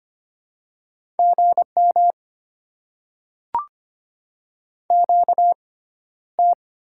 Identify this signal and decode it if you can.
Morse code, heard as a keyed tone, about 25 wpm, spaced as for 13 wpm, 700 Hz. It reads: GM QT
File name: Top 20 QSO Elements - 6x Character Spacing - 25wpm